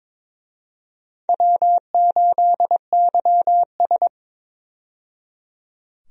Morse code, heard as a keyed tone, about 23 wpm, 700 Hz. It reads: W8YS
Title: USA Call Signs - 4 Characters - 22wpm